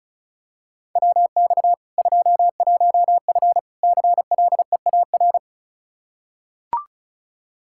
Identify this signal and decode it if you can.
Morse code, heard as a keyed tone, about 36 wpm, 700 Hz. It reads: WX 21F CLEAR
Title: K1USN SST QSO Practice 35wpm